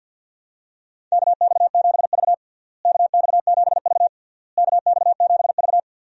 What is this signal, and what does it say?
Morse code, heard as a keyed tone, about 49 wpm, 700 Hz. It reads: KX7V KX7V KX7V
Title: USA Call Signs - 4 Characters - Repeated 3x - 50wpm